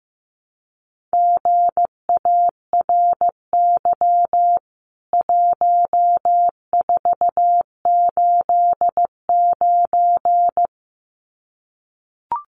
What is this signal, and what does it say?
Morse code, heard as a keyed tone, about 15 wpm, 700 Hz. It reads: GARY 1489
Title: CWT Exchange 15wpm